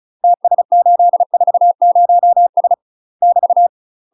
Morse code, heard as a keyed tone, about 36 wpm, 700 Hz. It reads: TS840S <BT>
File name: UK MASTER QSOs TRAINING / 35WPM / UK MASTER QSOs 0023